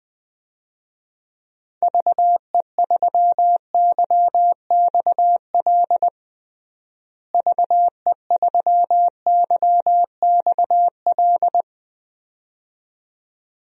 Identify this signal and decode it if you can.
Morse code, heard as a keyed tone, about 20 wpm, 700 Hz. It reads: VE3YXL VE3YXL
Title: DXCC-100 Call Signs - 6 Characters - Repeated 2x - 20wpm